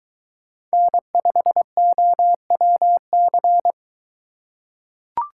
Morse code, heard as a keyed tone, about 22 wpm, 700 Hz. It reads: N5OWC